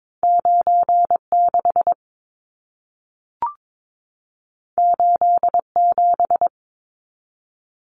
Morse code, heard as a keyed tone, about 22 wpm, 700 Hz. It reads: 96 87
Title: Numbers - 2-Digit 22wpm